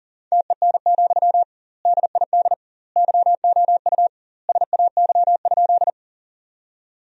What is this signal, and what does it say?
Morse code, heard as a keyed tone, about 40 wpm, 700 Hz. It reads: TEN, DID YOU SAY?
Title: 2x Word-Spacing - Sentences from Top 400 Words - 40wpm